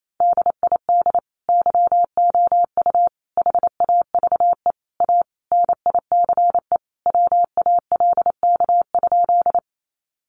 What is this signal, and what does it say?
Morse code, heard as a keyed tone, about 28 wpm, 700 Hz. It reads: DID YOU HAVE A NICE WALK?
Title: Sentences from Top 1100 Words 28wpm